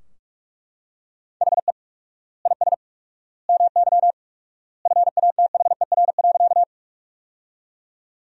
Morse code, heard as a keyed tone, about 48 wpm, 700 Hz. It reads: HE IS MY FATHER.
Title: 4x Word-Spacing - Sentences from Top 300 Words - 45wpm